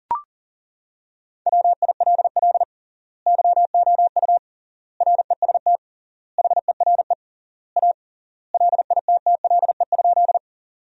Morse code, heard as a keyed tone, about 37 wpm, 700 Hz. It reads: WILL YOU REST HERE A LITTLE?
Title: 3x Word-Spacing - Sentences from Top 500 Words - 40wpm